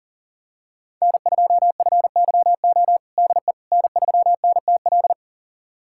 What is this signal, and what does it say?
Morse code, heard as a keyed tone, about 39 wpm, 700 Hz. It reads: N2FYO DE N3NTL
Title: USA Callsign DE Callsign - 5 Characters - 40wpm